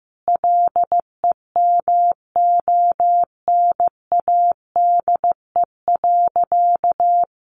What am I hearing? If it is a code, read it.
Morse code, heard as a keyed tone, about 15 wpm, 700 Hz. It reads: LEMONADE.
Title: Additional Easy Phrases 15wpm